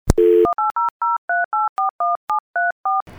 DTMF, heard as telephone touch tones, tones of about 180 ms. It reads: 18003841734